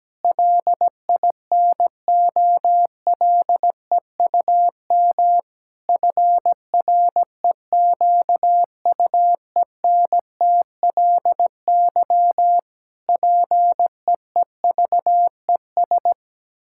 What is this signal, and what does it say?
Morse code, heard as a keyed tone, about 17 wpm, 700 Hz. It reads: LINOLEUM FREQUENTLY PEEVES
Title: Pangram Challenge v2 - 17wpm